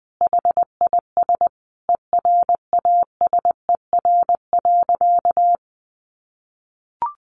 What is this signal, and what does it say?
Morse code, heard as a keyed tone, about 20 wpm, 700 Hz. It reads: HIS ERASER.
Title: Additional Easy Phrases 20wpm